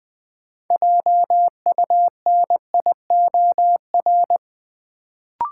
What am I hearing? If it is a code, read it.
Morse code, heard as a keyed tone, about 20 wpm, 700 Hz. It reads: JUNIOR
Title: Sentences from Top 1000 Words - Review Part 7 - New Word Drill - 20wpm